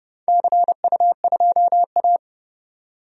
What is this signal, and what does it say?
Morse code, heard as a keyed tone, about 30 wpm, 700 Hz. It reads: CU2A
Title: DXCC-100 Call Signs - 4 Characters - 30wpm